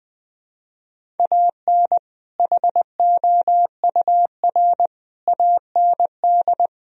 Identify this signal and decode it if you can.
Morse code, heard as a keyed tone, about 20 wpm, 700 Hz. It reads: AN HOUR AND
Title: Speed-Racing - Sets of 5 Words 17wpm